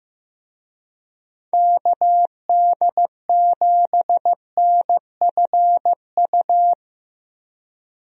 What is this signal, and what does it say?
Morse code, heard as a keyed tone, about 15 wpm, 700 Hz. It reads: KD7NFU
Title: USA Call Signs - 6 Characters - Repeated 2x - 15wpm